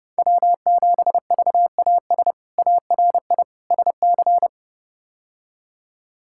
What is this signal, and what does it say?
Morse code, heard as a keyed tone, about 32 wpm, 700 Hz. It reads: W7VAS ARI SC
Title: K1USN SST QSO Practice - 30wpm